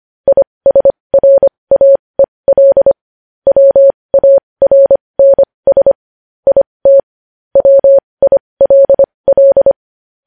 Morse code, heard as a keyed tone, about 25 wpm, spaced as for 20 wpm, 550 Hz. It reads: ISRAEL WARNS IT WILL
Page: News Headlines in Morse Code at 20 WPM